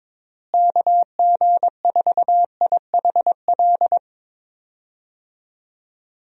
Morse code, heard as a keyed tone, about 22 wpm, 700 Hz. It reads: KG4IHL
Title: USA Call Signs - 6 Characters - Repeated 2x - 22wpm